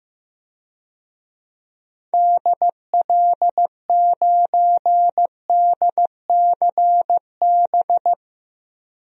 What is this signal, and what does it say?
Morse code, heard as a keyed tone, about 15 wpm, 700 Hz. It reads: DL9DCB
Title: DXCC-100 Call Signs - 6 Characters - 15wpm